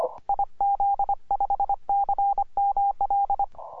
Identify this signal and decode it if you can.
Morse code, heard as a keyed tone, about 25 wpm, 800 Hz. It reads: IZ5CML
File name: simple-file-list / ZL7E 17MCW